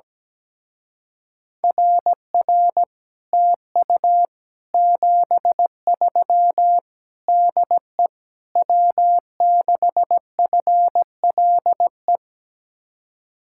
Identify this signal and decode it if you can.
Morse code, heard as a keyed tone, about 17 wpm, 700 Hz. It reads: RR TU 73 DE W6FLE
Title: Speed-Racing Common QSO Phrases - 17wpm